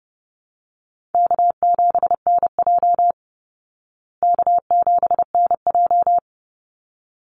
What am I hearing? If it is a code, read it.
Morse code, heard as a keyed tone, about 30 wpm, 700 Hz. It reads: K7NJ K7NJ
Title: Easy Call Signs Repeated 2x - 30wpm